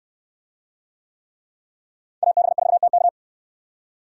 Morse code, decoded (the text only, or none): AB4TK